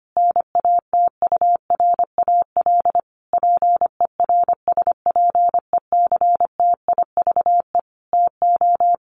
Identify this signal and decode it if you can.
Morse code, heard as a keyed tone, about 25 wpm, 700 Hz. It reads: NATURAL PERSPECTIVE TO
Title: Sentences from Top 1200 Words 25wpm